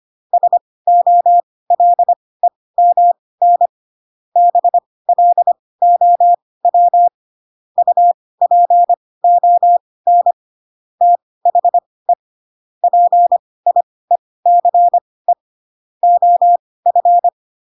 Morse code, heard as a keyed tone, about 25 wpm, spaced as for 18 wpm, 700 Hz. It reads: SOLEMN BLOW UPON THE PIECE OF
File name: Pinocchio / Pinocchio ch1 12 25 ik1hge